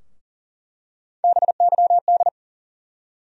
Morse code, heard as a keyed tone, about 40 wpm, 700 Hz. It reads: BYD
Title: USA Call Sign Postfixes - 3 Letters 40wpm